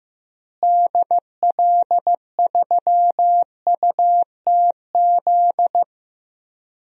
Morse code, heard as a keyed tone, about 15 wpm, 700 Hz. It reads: DL3UTZ
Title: DXCC-100 Call Signs - 6 Characters - Repeated 3x - 15wpm